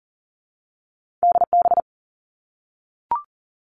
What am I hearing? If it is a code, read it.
Morse code, heard as a keyed tone, about 42 wpm, 700 Hz. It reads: DB
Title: Two Challenging Letter-Number Combinations 40wpm